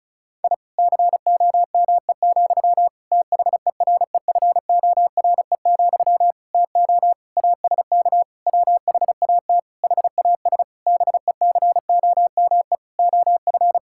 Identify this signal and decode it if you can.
Morse code, heard as a keyed tone, about 34 wpm, 700 Hz. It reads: I COME, THEREFORE, TO ASK WHAT HAS BECOME OF
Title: Sentences from Top 1000 Words - Review Part 5 - 35wpm